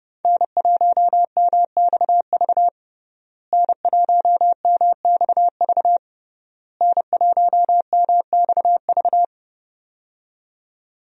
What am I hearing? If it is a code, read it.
Morse code, heard as a keyed tone, about 30 wpm, 700 Hz. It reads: N1MXV N1MXV N1MXV
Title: USA Call Signs - 5 Characters - Repeated 3x - 30wpm